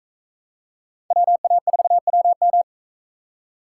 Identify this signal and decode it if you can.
Morse code, heard as a keyed tone, about 44 wpm, 700 Hz. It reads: WA4WM